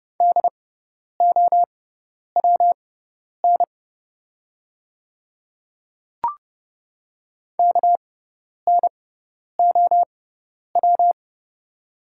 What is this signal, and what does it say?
Morse code, heard as a keyed tone, about 30 wpm, spaced as for 10 wpm, 700 Hz. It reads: DOWN KNOW
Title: 6x Character Spacing - Top 100 Words - 30wpm